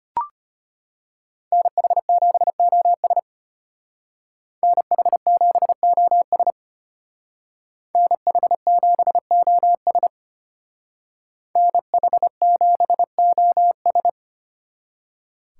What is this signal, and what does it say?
Morse code, heard as a keyed tone, about 32 wpm, 700 Hz. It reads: NH7OS NH7OS NH7OS NH7OS